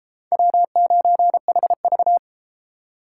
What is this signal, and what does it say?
Morse code, heard as a keyed tone, about 31 wpm, 700 Hz. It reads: W9HV